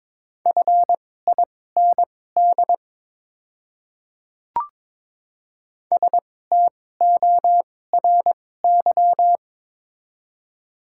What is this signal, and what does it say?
Morse code, heard as a keyed tone, about 22 wpm, spaced as for 16 wpm, 700 Hz. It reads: FIND STORY
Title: 2x Character Spacing - Top 300 Words - 22wpm